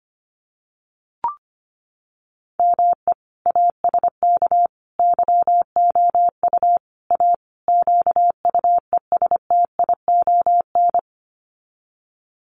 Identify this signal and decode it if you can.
Morse code, heard as a keyed tone, about 25 wpm, 700 Hz. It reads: ME ASK YOU A QUESTION
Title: Sets of 5 Words Encore2 25wpm